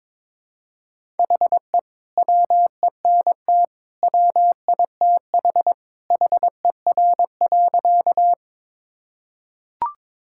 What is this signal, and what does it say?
Morse code, heard as a keyed tone, about 22 wpm, 700 Hz. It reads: HE WENT WITH HER.
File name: practice / 22wpm / Sentences with 2-Word Phrases - 22wpm